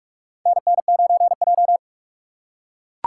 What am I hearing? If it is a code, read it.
Morse code, heard as a keyed tone, about 44 wpm, 700 Hz. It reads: NN9J
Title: USA Call Signs - 4 Characters - 4x Character Spacing - 45wpm